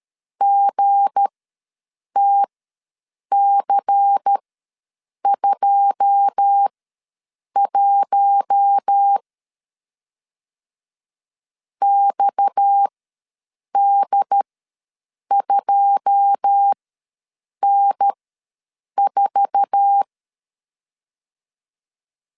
Morse code, heard as a keyed tone, about 13 wpm, spaced as for 7 wpm, 800 Hz. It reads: GTC21 XD2N4
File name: LEZIONI / 9 2 ws 12wpm 3406